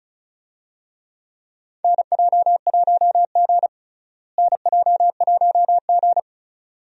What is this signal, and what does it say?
Morse code, heard as a keyed tone, about 36 wpm, 700 Hz. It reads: NJ1G NJ1G
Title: USA Call Signs - 4 Characters - Repeated 2x - 35wpm